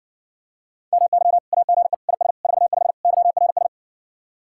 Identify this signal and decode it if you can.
Morse code, heard as a keyed tone, about 60 wpm, 700 Hz. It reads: MY AGE IS 45 YRS